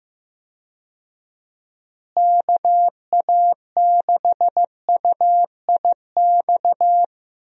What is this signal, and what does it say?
Morse code, heard as a keyed tone, about 15 wpm, 700 Hz. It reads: KA6UIX
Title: USA Call Signs - 6 Characters - Repeated 2x - 15wpm